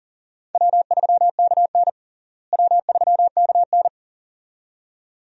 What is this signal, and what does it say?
Morse code, heard as a keyed tone, about 40 wpm, 700 Hz. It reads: W3KN W3KN
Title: USA Call Signs - 4 Characters - Repeated 2x - 40wpm